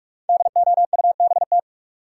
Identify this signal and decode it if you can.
Morse code, heard as a keyed tone, about 46 wpm, 700 Hz. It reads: DOUBT
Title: Sets of 5 Words - New Word Drill - 45wpm